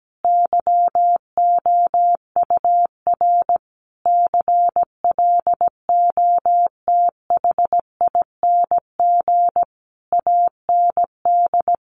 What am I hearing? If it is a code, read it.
Morse code, heard as a keyed tone, about 17 wpm, 700 Hz. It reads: YOUR CLOTHING AND
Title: A Sound Of Thunder 17wpm